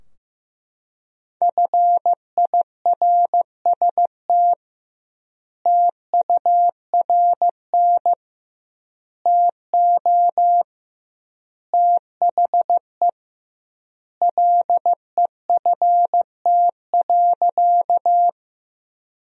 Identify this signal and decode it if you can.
Morse code, heard as a keyed tone, about 15 wpm, 700 Hz. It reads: FIRST TURN TO THE LEFT.
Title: 2x Word-Spacing - Sentences from Top 300 Words - 15wpm